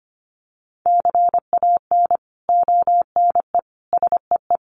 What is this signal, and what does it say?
Morse code, heard as a keyed tone, about 25 wpm, 700 Hz. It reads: CAN ONE SEE
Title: Sentences from Top 100 Words 25wpm